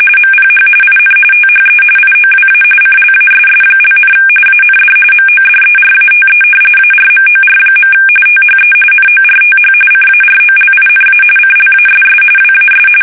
RTTY, about 45.45 baud. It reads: MPH IN THE IOWA CLASS), AND MODERATE ARMOR. THE NORTH CAROLINA CLASS WAS
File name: fsk850 / battleships